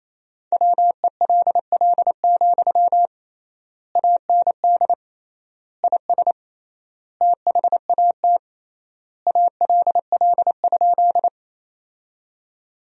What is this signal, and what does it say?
Morse code, heard as a keyed tone, about 28 wpm, 700 Hz. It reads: WELL, AND IS THAT ALL?
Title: 3x Word-Spacing - Sentences from Top 200 Words - 28wpm